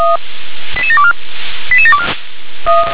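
Five-tone selective call (ZVEI), heard as quick runs of five tones, tones of about 70 ms. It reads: 88724 88724
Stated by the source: ZVEI five-tone selective call 88709 received by SDR.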